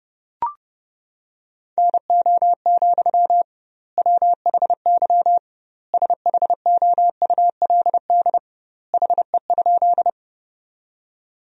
Morse code, heard as a keyed tone, about 30 wpm, 700 Hz. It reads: NO, WHY SHOULD HE?
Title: 2x Word-Spacing - Sentences from Top 300 Words - 30wpm